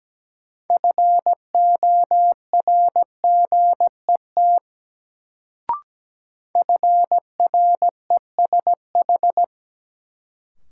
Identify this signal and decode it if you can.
Morse code, heard as a keyed tone, about 17 wpm, 700 Hz. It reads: FORGET FRESH
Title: Sentences from Top 1200 Words - New Word Drill - 17wpm